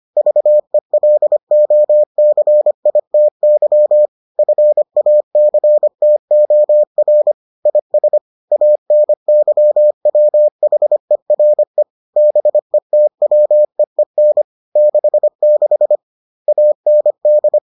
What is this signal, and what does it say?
Morse code, heard as a keyed tone, about 25 wpm, 600 Hz. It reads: VELOCITY FACTOR IS ANYWHERE BETWEEN 66 AND